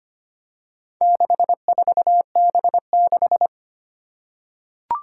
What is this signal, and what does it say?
Morse code, heard as a keyed tone, about 25 wpm, 700 Hz. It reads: 64B6
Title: Four Challenging Letter-Number Combinations 25wpm